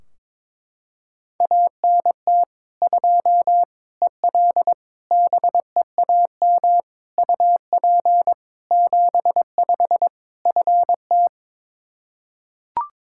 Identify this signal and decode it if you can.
Morse code, heard as a keyed tone, about 22 wpm, 700 Hz. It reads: ANT 2 EL BEAM UP 75 FT
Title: K1USN SST QSO Practice 22wpm